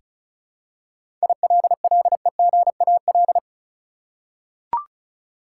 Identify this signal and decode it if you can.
Morse code, heard as a keyed tone, about 34 wpm, 700 Hz. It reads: ILLEGAL
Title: Seven Letter Words - 35wpm